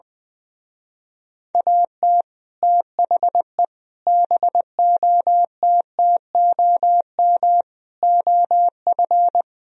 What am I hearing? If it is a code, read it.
Morse code, heard as a keyed tone, about 20 wpm, 700 Hz. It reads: AT THE BOTTOM OF
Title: Speed-Racing - Sets of 5 Words 20wpm